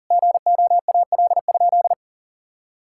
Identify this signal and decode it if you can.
Morse code, heard as a keyed tone, about 40 wpm, 700 Hz. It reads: GOAL?
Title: Sentences from Top 1000 Words - Review Part 6 - 40wpm